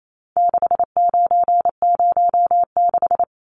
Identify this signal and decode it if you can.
Morse code, heard as a keyed tone, about 28 wpm, 700 Hz. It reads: 6906
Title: Numbers - 4-Digit 28wpm